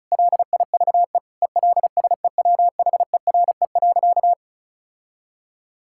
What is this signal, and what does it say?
Morse code, heard as a keyed tone, about 34 wpm, 700 Hz. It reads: LIVE ELSEWHERE.
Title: Sentences from Top 1600 Words 35wpm